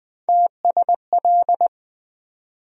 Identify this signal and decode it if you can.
Morse code, heard as a keyed tone, about 21 wpm, 700 Hz. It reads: TSL
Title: USA Call Sign Postfixes - 3 Letters 20wpm